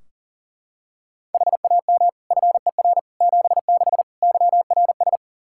Morse code, heard as a keyed tone, about 40 wpm, 700 Hz. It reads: HAM FER 76 YRS
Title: K1USN SST QSO Practice 40wpm